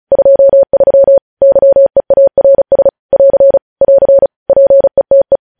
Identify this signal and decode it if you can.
Morse code, heard as a keyed tone, about 36 wpm, 550 Hz. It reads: 23 YEARS <AR> <AR> PETE
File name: podcast / News headlines-1x-35x35